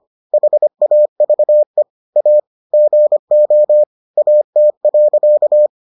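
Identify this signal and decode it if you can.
Morse code, heard as a keyed tone, about 25 wpm, 600 Hz. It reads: HAVE A GO AT.
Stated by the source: Morse edition of the Foundations of Amateur Radio episodes.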